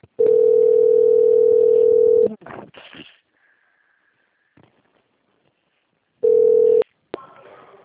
Ringback tone (North America), 440 + 480 Hz.